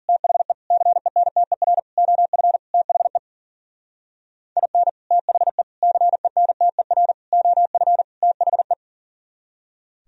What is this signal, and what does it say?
Morse code, heard as a keyed tone, about 51 wpm, 700 Hz. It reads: THE CENTER OF THE IN THE CENTER OF THE